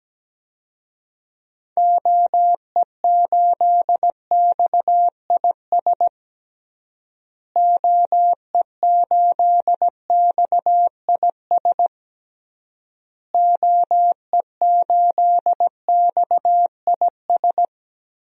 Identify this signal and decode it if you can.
Morse code, heard as a keyed tone, about 17 wpm, 700 Hz. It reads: OE8XIS OE8XIS OE8XIS
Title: DXCC-100 Call Signs - 6 Characters - Repeated 3x - 17wpm